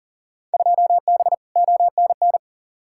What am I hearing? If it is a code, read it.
Morse code, heard as a keyed tone, about 40 wpm, 700 Hz. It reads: 2B ONN
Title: Field Day Exchanges Encore - 40wpm